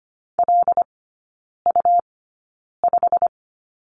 Morse code, heard as a keyed tone, about 25 wpm, spaced as for 9 wpm, 700 Hz. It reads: LU5
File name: Lesson 17e - Letter u - Three Characters 25wpm